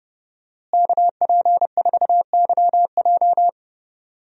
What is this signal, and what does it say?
Morse code, heard as a keyed tone, about 30 wpm, 700 Hz. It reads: KP4YJ
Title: DXCC-100 Call Signs - 5 Characters - Repeated 2x - 30wpm